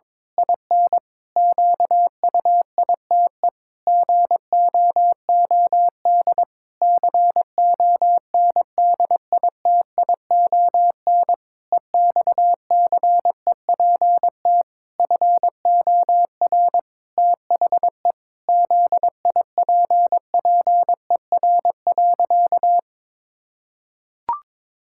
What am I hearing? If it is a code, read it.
Morse code, heard as a keyed tone, about 22 wpm, 700 Hz. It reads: IN QUITE GOOD CONDITION EXCEPT FOR THE ZIPPER.
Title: Pangram Challenge v2 - 22wpm